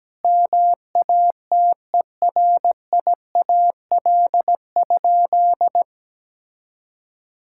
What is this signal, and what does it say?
Morse code, heard as a keyed tone, about 17 wpm, 700 Hz. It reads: MATERIAL?
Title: Sentences from Top 1000 Words - Review Part 5 - 17wpm